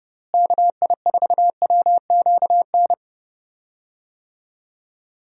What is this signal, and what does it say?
Morse code, heard as a keyed tone, about 32 wpm, 700 Hz. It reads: KI4WQN
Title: USA Call Signs - 6 Characters - Repeated 2x - 30wpm